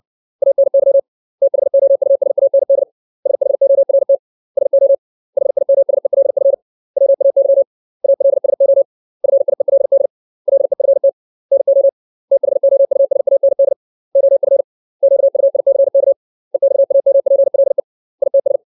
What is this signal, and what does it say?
Morse code, heard as a keyed tone, about 60 wpm, 550 Hz. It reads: ANY THOUSAND SHORT SO HIMSELF WAY ALSO FIELD LEFT NO THOUSAND OF QUICK EXAMPLE ITS
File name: 632 words / 632 Words 60WPM